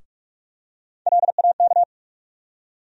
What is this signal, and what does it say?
Morse code, heard as a keyed tone, about 43 wpm, 700 Hz. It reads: LAK
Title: USA Call Sign Postfixes - 3 Letters 45wpm